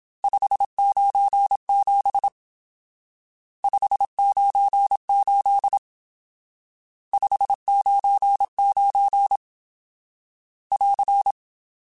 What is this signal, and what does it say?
Morse code, heard as a keyed tone, about 27 wpm, 800 Hz. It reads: 597 598 599 <AR>
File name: RSTs / RST-mp3s / Various RST @25